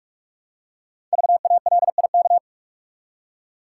Morse code, heard as a keyed tone, about 44 wpm, 700 Hz. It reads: VALIK
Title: Code Groups - 5 Letters - 45wpm